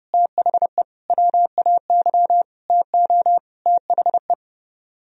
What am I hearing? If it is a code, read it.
Morse code, heard as a keyed tone, about 29 wpm, 700 Hz. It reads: THE WAY TO THE